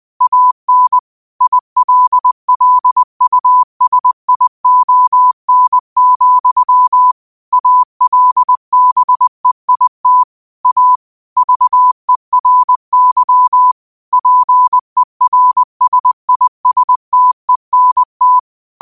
Morse code, heard as a keyed tone, about 20 wpm, 1000 Hz. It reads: AN ILLUSION, ALBEIT A VERY PERSISTENT